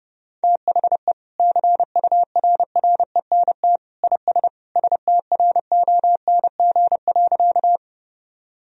Morse code, heard as a keyed tone, about 30 wpm, 700 Hz. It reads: THE CURRENT IS STRONG.
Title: Sentences from Top 1000 Words - Review Part 4 - 30wpm